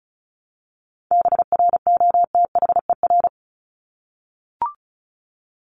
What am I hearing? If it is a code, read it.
Morse code, heard as a keyed tone, about 34 wpm, 700 Hz. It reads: BROTHER
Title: Top 1751-2000 Words 35wpm